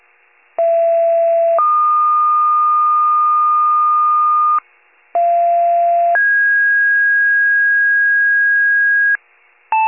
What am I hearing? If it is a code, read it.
Two-tone page, heard as a one-second tone, then a three-second tone, about 670 then 1220 Hz, about 690 then 1650 Hz, with noise. It is heard from 0.6 to 9.2 s.
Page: 41 RESCUE DUTY